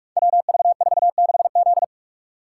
Morse code, heard as a keyed tone, about 44 wpm, 700 Hz. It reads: WV46Z
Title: Code Groups - 5 Letter:Numbers - 45wpm